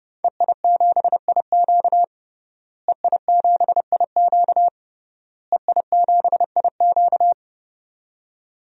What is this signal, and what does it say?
Morse code, heard as a keyed tone, about 31 wpm, 700 Hz. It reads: EI7IQ EI7IQ EI7IQ
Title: DXCC-100 Call Signs - 5 Characters - Repeated 3x - 30wpm